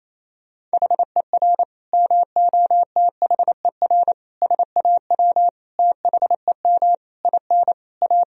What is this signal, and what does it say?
Morse code, heard as a keyed tone, about 28 wpm, 700 Hz. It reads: HER MOTHER SAW THEM IN A
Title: Sentences from Top 1000 Words - Review Part 3 - 28wpm